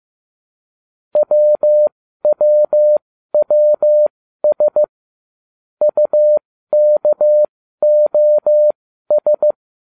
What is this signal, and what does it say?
Morse code, heard as a keyed tone, about 15 wpm, spaced as for 12 wpm, 600 Hz. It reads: WWWS UKOS